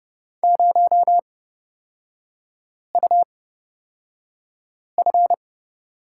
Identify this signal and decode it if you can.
Morse code, heard as a keyed tone, about 30 wpm, spaced as for 5 wpm, 700 Hz. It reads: 0UF
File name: Lesson 38e - Letter q - Three Characters 30wpm